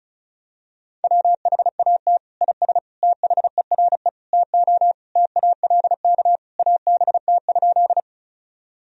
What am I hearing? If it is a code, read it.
Morse code, heard as a keyed tone, about 36 wpm, 700 Hz. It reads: WHAT IS THERE TO TALK ABT?
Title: Sentences from Top 400 Words and Abbreviations 35wpm